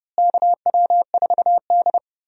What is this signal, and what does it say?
Morse code, heard as a keyed tone, about 30 wpm, 700 Hz. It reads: KW4D
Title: USA Call Signs - 4 Characters - Repeated 3x - 30wpm